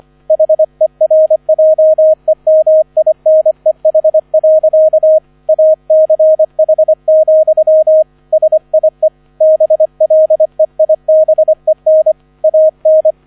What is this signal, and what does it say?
Morse code, heard as a keyed tone, about 25 wpm, 630 Hz, with radio noise. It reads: HERJEMINEH. ACH, SIE BLEIBEN AN